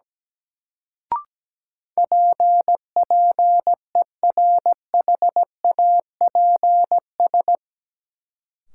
Morse code, heard as a keyed tone, about 17 wpm, 700 Hz. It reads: P PERHAPS
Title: Sentences from Top 1100 Words - New Word Drill Alt - 17wpm